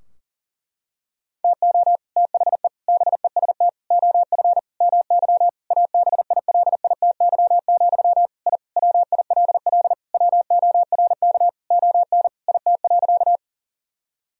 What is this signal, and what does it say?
Morse code, heard as a keyed tone, about 37 wpm, 700 Hz. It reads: TO THE BEST OF MY ABILITY, I WILL WORK ON IT.
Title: Sentences from Top 1000 Words - Review Part 3 - 40wpm